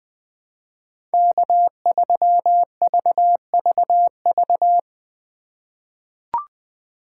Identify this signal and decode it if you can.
Morse code, heard as a keyed tone, about 20 wpm, 700 Hz. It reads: K3VVV
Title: USA Call Signs - 5 Characters - 2x Character Spacing - 20wpm